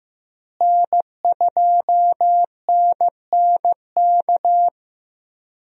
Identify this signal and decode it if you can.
Morse code, heard as a keyed tone, about 15 wpm, 700 Hz. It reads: N2NNK
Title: USA Call Signs - 5 Characters - Repeated 2x - 15wpm